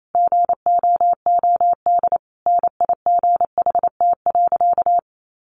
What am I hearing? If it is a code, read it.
Morse code, heard as a keyed tone, about 28 wpm, 700 Hz. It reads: GOOD NIGHT.
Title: Sentences from Top 300 Words 28wpm